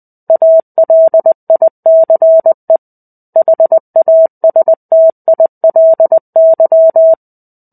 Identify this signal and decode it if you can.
Morse code, heard as a keyed tone, about 20 wpm, 650 Hz. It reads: ALICE HASTILY
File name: 05-Chapter-05 20 WPM